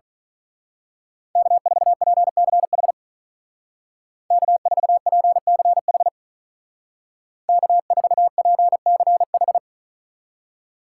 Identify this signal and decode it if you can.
Morse code, heard as a keyed tone, about 40 wpm, 700 Hz. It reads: K4PCH K4PCH K4PCH
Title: Speed Racing - USA 1x3 Call Signs 35wpm